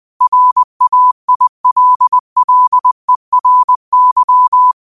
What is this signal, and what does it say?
Morse code, heard as a keyed tone, about 20 wpm, 1000 Hz. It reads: RAILLERY
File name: qotdfiles / 20wpm